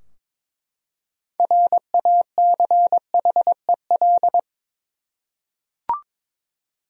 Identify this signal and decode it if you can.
Morse code, heard as a keyed tone, about 22 wpm, 700 Hz. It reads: RACHEL
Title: USA Common Names - 22wpm